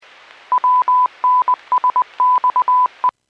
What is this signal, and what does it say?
Morse code, heard as a keyed tone, about 20 wpm, 1000 Hz. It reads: WNSXE
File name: WPS / 00440 17-57 WPS 451.25 WPS